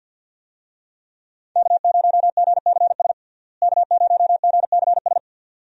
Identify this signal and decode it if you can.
Morse code, heard as a keyed tone, about 48 wpm, 700 Hz. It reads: K0GCS K0GCS
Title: USA Call Signs - 5 Characters - Repeated 2x - 50wpm